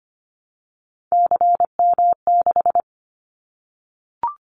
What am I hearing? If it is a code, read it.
Morse code, heard as a keyed tone, about 25 wpm, 700 Hz. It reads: CM6
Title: Callsign Prefixes for Top 100 DXCC - 25wpm